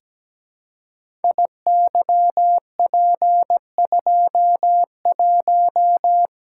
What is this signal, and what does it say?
Morse code, heard as a keyed tone, about 17 wpm, 700 Hz. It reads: IYP21
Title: Code Groups - 5 Letter:Numbers - 17wpm